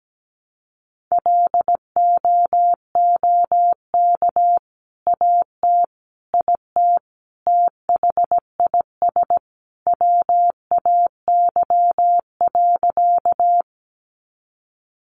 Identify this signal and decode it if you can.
Morse code, heard as a keyed tone, about 17 wpm, 700 Hz. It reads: LOOK AT IT THIS WAY.
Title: Sentences from Top 100 Words 17wpm